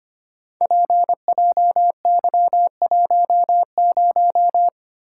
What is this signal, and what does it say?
Morse code, heard as a keyed tone, about 25 wpm, 700 Hz. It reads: PJY10
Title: Code Groups - 5 Letter:Numbers - 25wpm